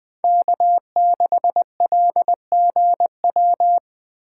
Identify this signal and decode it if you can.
Morse code, heard as a keyed tone, about 21 wpm, 700 Hz. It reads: K6LGW